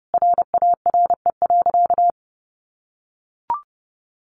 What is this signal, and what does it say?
Morse code, heard as a keyed tone, about 30 wpm, 700 Hz. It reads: RARE.
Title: Sentences from Top 1100 Words 30wpm